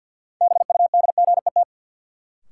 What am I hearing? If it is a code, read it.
Morse code, heard as a keyed tone, about 51 wpm, 700 Hz. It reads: BUDGET
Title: Sentences from Top 1200 Words - New Word Drill Alt - 50wpm